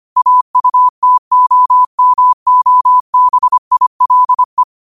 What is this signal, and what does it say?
Morse code, heard as a keyed tone, about 25 wpm, 1000 Hz. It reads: AUTOMOBILE
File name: qotdfiles / 25wpm